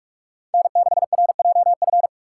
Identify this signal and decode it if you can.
Morse code, heard as a keyed tone, about 46 wpm, 700 Hz. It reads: N6RJF
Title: USA Call Signs - 6x Character Spacing - 45wpm